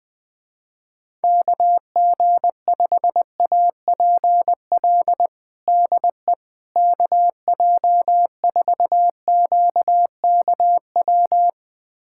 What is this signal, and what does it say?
Morse code, heard as a keyed tone, about 21 wpm, 700 Hz. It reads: KG5APL DE KJ4QKW
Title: USA Callsign DE Callsign 20wpm